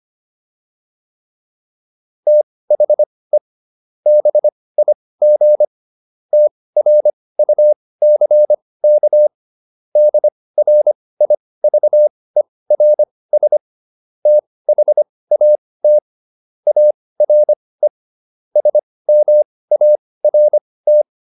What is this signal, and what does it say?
Morse code, heard as a keyed tone, about 25 wpm, spaced as for 18 wpm, 600 Hz. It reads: THE BIG TRUCK DRIVERS THAT ARE SMART